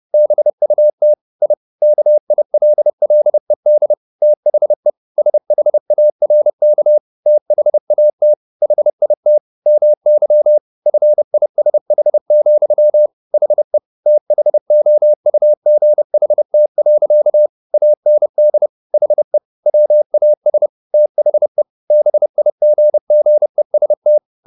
Morse code, heard as a keyed tone, about 30 wpm, 600 Hz. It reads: BUT I KILLED THE SHARK THAT HIT MY FISH, HE THOUGHT. AND HE WAS THE BIGGEST